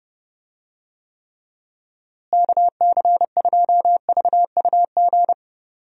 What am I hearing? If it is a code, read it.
Morse code, heard as a keyed tone, about 30 wpm, 700 Hz. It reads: KC2VUG
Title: USA Call Signs - 6 Characters - Repeated 3x - 30wpm